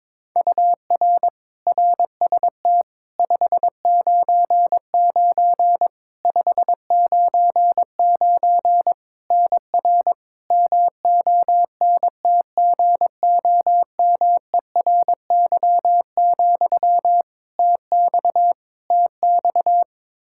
Morse code, heard as a keyed tone, about 22 wpm, 700 Hz. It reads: UR RST 599 599 NR MONTGOMERY, TX TX
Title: Practice QSO - Exchange 1 - Beginning - 22wpm